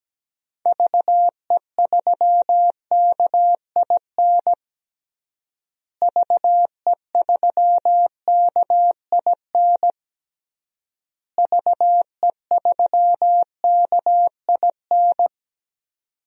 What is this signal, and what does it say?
Morse code, heard as a keyed tone, about 17 wpm, 700 Hz. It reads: VE3KIN VE3KIN VE3KIN
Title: DXCC-100 Call Signs - 6 Characters - Repeated 3x - 17wpm